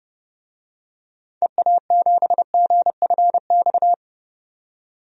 Morse code, heard as a keyed tone, about 30 wpm, 700 Hz. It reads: EA7GFX
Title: DXCC-100 Call Signs - 6 Characters - 30wpm